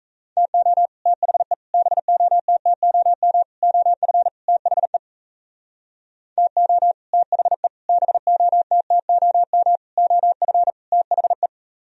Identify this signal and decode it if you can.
Morse code, heard as a keyed tone, about 36 wpm, 700 Hz. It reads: TO THE BOTTOM OF THE TO THE BOTTOM OF THE